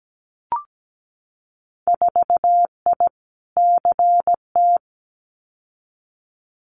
Morse code, heard as a keyed tone, about 17 wpm, 700 Hz. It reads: 4I CT
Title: Winter Field Day Exchanges Encore 17wpm